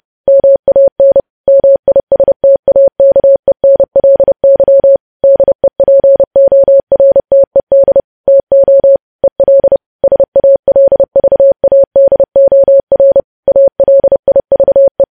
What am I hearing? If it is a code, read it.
Morse code, heard as a keyed tone, about 30 wpm, 550 Hz. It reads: MAN MISTAKENLY DEPORTED TO EL SALVADOR ALIVE
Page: news headlines in morse code at 30 wpm